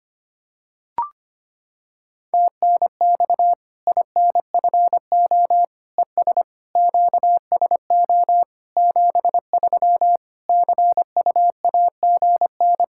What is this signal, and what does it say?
Morse code, heard as a keyed tone, about 25 wpm, 700 Hz. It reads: TNX INFO ES QSO 73 CUAGN
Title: Practice QSO - Exchange 4 - Ending - 25wpm